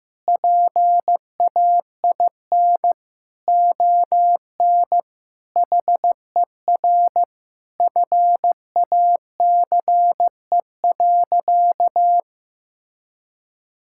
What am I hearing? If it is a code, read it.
Morse code, heard as a keyed tone, about 15 wpm, 700 Hz. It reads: PAIN ON HER FACE.
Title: Sentences from Top 1600 Words 15wpm